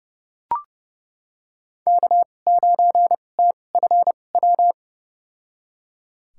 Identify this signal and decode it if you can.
Morse code, heard as a keyed tone, about 32 wpm, spaced as for 22 wpm, 700 Hz. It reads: K9TFW
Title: USA Call Signs - 5 Characters - 2x Character Spacing - 30wpm